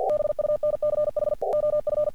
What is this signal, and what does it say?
Morse code, heard as a keyed tone, about 44 wpm, 600 Hz. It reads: BUNCH OF
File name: archived / CRSnet